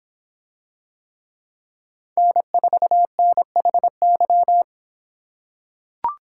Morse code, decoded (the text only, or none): N4NHY